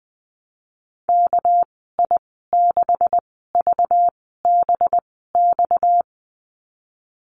Morse code, heard as a keyed tone, about 20 wpm, spaced as for 15 wpm, 700 Hz. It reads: KI6VBX
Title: 231 Lesson 37g - Letter x - Call Signs 20wpm